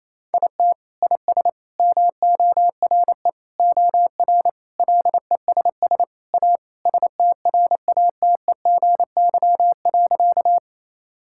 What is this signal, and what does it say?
Morse code, heard as a keyed tone, about 28 wpm, 700 Hz. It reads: IT IS MORE OR LESS A STRATEGY.
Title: Sentences from Top 1000 Words - Review Part 5 - 28wpm